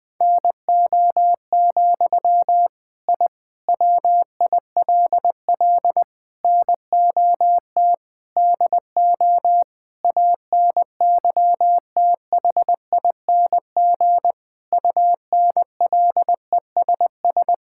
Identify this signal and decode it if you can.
Morse code, heard as a keyed tone, about 20 wpm, 700 Hz. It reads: NO, I WILL NOT DO ANYTHING UNLESS
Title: Sentences from Top 1000 Words - Review Part 6 - 20wpm